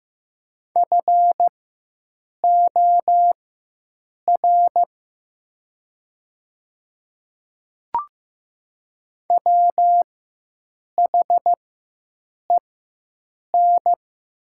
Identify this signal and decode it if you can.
Morse code, heard as a keyed tone, about 15 wpm, spaced as for 7 wpm, 700 Hz. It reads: FOR WHEN
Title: 4x Character Spacing - Top 300 Words - 15wpm